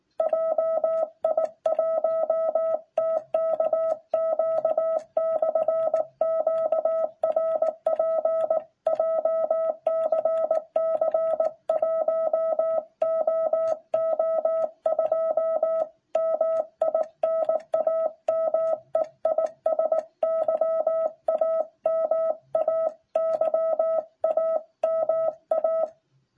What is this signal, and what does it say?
Morse code, heard as a keyed tone, about 16 wpm, 660 Hz. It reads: JI1TKQ/QRP JCC 1OO2 MI NAME IS YAMA YAMA